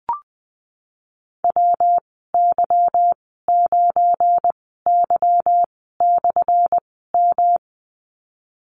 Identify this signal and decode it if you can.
Morse code, heard as a keyed tone, about 19 wpm, spaced as for 14 wpm, 700 Hz. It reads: WY9Y/M
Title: 179 Lesson 29g - Slash - Call Signs 20wpm